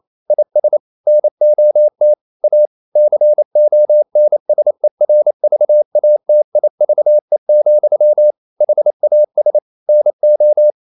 Morse code, heard as a keyed tone, about 28 wpm, 600 Hz. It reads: IS NOT A CONSERVATIVE, HAS NO
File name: CW / Practice / EnglishQuotations0228